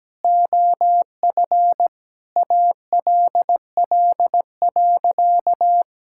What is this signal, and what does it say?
Morse code, heard as a keyed tone, about 17 wpm, 700 Hz. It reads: OF ALL.
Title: Sentences from Top 1000 Words - Review Part 6 - 17wpm